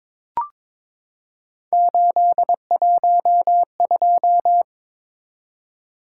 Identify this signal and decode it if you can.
Morse code, heard as a keyed tone, about 22 wpm, 700 Hz. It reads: 812
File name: practice / 22wpm / Three Letter-Number Combinations - 22wpm